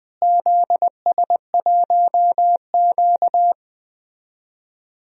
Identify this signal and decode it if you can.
Morse code, heard as a keyed tone, about 20 wpm, 700 Hz. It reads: ZS1Q
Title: DXCC-100 Call Signs - 4 Characters - Repeated 2x - 20wpm